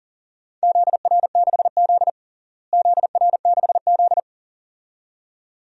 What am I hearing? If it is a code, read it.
Morse code, heard as a keyed tone, about 40 wpm, 700 Hz. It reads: ZR6Z ZR6Z
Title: DXCC-100 Call Signs - 4 Characters - Repeated 2x - 40wpm